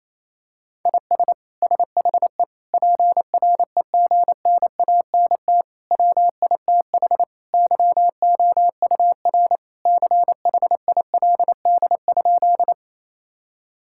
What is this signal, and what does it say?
Morse code, heard as a keyed tone, about 27 wpm, 700 Hz. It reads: IS SHE PREGNANT WITH YOUR CHILD?
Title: Sentences from Top 1000 Words - Review Part 6 - 28wpm